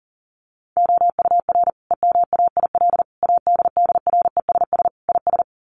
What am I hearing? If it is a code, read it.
Morse code, heard as a keyed tone, about 40 wpm, 700 Hz. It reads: OUR EMAIL ADDRESS IS
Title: Sets of 5 Words Encore2 40wpm